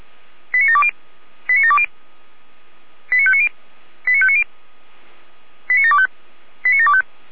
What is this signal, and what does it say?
Five-tone selective call (ZVEI), heard as quick runs of five tones, tones of about 70 ms. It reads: 89720 89720 89590 89590 89725 89725